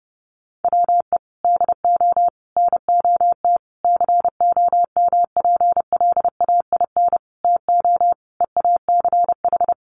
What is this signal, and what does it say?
Morse code, heard as a keyed tone, about 30 wpm, 700 Hz. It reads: WE DO NOT COMPLAIN TO EACH
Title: Sentences from Top 1300 Words 30wpm